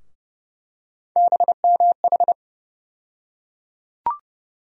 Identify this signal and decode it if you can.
Morse code, heard as a keyed tone, about 30 wpm, 700 Hz. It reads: BMH
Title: USA Call Sign Postfixes - 3 Letters 30wpm